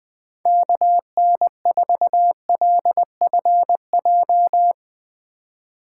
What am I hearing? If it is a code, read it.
Morse code, heard as a keyed tone, about 20 wpm, 700 Hz. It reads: KN4LFJ
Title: USA Call Signs - 6 Characters - 3x Character Spacing - 20wpm